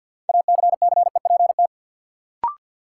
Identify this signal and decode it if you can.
Morse code, heard as a keyed tone, about 52 wpm, 700 Hz. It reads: ACCEPT
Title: Sentences from Top 1300 Words - New Word Drill Alt - 50wpm